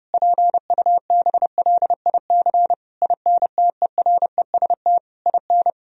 Morse code, heard as a keyed tone, about 30 wpm, 700 Hz. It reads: PUBLIC INTEREST IN
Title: Sentences from Top 1000 Words - Review Part 3 - 30wpm